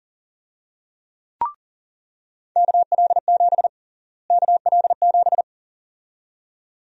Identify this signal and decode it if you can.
Morse code, heard as a keyed tone, about 37 wpm, 700 Hz. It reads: KL7 KL7
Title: USA Call Sign Prefixes - Repeated 2x - 40wpm